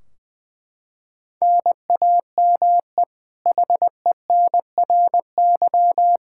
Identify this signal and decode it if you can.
Morse code, heard as a keyed tone, about 21 wpm, 700 Hz. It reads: NAME HENRY
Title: Practice QSO - Exchange 1 - Beginning - 20wpm